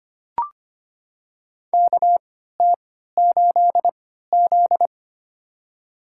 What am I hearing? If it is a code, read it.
Morse code, heard as a keyed tone, about 25 wpm, spaced as for 14 wpm, 700 Hz. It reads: KT8Z
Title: USA Call Signs - 3x Character Spacing - 25wpm